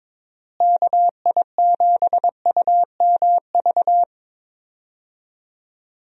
Morse code, heard as a keyed tone, about 22 wpm, 700 Hz. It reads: KI7UMV
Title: USA Call Signs - 6 Characters - Repeated 3x - 22wpm